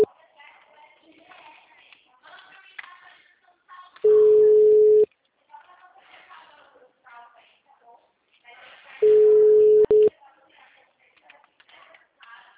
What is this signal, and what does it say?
Ringback tone (Europe, 425 Hz), with speech.